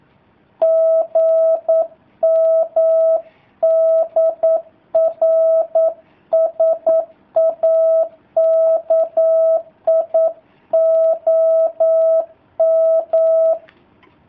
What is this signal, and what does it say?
Morse code, heard as a keyed tone, about 9 wpm, 650 Hz. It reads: GM DR SAKI OM
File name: jh8cbh / 2019cw / sp09- w